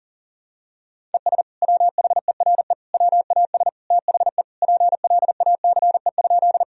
Morse code, heard as a keyed tone, about 37 wpm, 700 Hz. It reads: ES WHERE WAS THE PLACE?
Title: Sentences from Top 200 Words and Abbreviations 40wpm